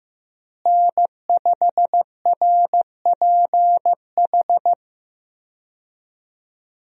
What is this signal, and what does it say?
Morse code, heard as a keyed tone, about 16 wpm, 700 Hz. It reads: N5RPH
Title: USA Call Signs - 5 Characters - Repeated 2x - 15wpm